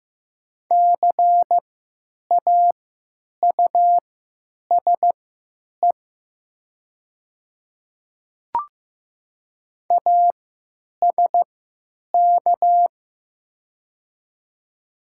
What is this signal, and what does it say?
Morse code, heard as a keyed tone, about 15 wpm, spaced as for 9 wpm, 700 Hz. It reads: CAUSE ASK
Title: 3x Character Spacing - Top 500 Words - 15wpm